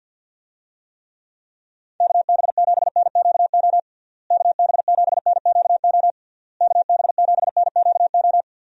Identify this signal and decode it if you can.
Morse code, heard as a keyed tone, about 50 wpm, 700 Hz. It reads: KB7NQO KB7NQO KB7NQO
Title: USA Call Signs - 6 Characters - Repeated 3x - 50wpm